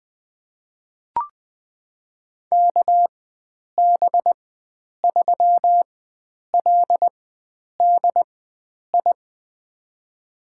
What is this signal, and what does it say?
Morse code, heard as a keyed tone, about 20 wpm, spaced as for 9 wpm, 700 Hz. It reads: KB3LDI
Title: USA Call Signs - 6 Characters - 4x Character Spacing - 20wpm